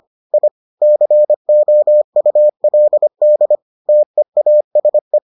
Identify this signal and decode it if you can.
Morse code, heard as a keyed tone, about 25 wpm, 600 Hz. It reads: I COULD TEASE